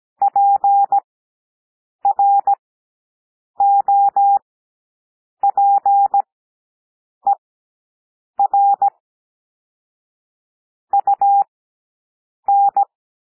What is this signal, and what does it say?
Morse code, heard as a keyed tone, about 17 wpm, spaced as for 7 wpm, 800 Hz. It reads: PROPER UN
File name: SOUND PLAIN 17 wpm